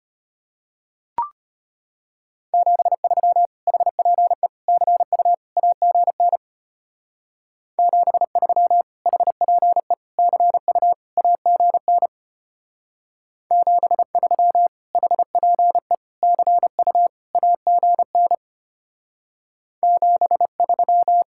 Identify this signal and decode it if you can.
Morse code, heard as a keyed tone, about 31 wpm, 700 Hz. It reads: 73 HPE CU AGN 73 HPE CU AGN 73 HPE CU AGN 73
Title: Speed-Racing Common QSO Phrases - 25wpm